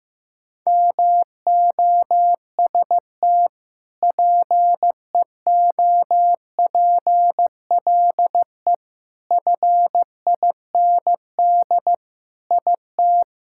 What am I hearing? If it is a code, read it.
Morse code, heard as a keyed tone, about 15 wpm, 700 Hz. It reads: MOST PEOPLE FIND IT
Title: Sentences from Top 1000 Words - Review Part 6 - 15wpm